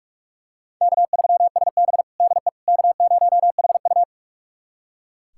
Morse code, heard as a keyed tone, about 43 wpm, 700 Hz. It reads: K3SB DE K0HU
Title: USA Callsign DE Callsign - 4 Characters - 45wpm